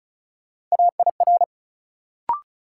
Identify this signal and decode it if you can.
Morse code, heard as a keyed tone, about 36 wpm, 700 Hz. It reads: AIR
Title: Sentences from Top 200 Words - New Word Drill - 35wpm